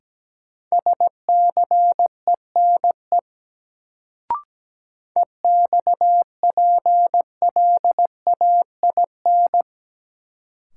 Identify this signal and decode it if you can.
Morse code, heard as a keyed tone, about 17 wpm, 700 Hz. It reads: SCENE EXPLAIN
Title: Sentences from Top 1100 Words - New Word Drill Alt - 17wpm